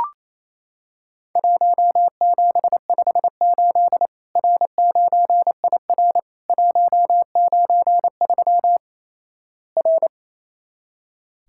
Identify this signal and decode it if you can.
Morse code, heard as a keyed tone, about 28 wpm, 700 Hz. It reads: 1758 R9IR 193 R
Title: Worked All Europe DX Contest - QTC Exchange - 28wpm